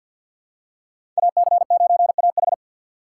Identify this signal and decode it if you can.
Morse code, heard as a keyed tone, about 52 wpm, 700 Hz. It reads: AC9AH
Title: USA Call Signs - 5 Characters - 6x Character Spacing - 50wpm